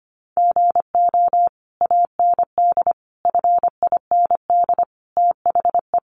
Morse code, heard as a keyed tone, about 25 wpm, 700 Hz. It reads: GO AND FIND THE
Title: Sentences from Top 1200 Words 25wpm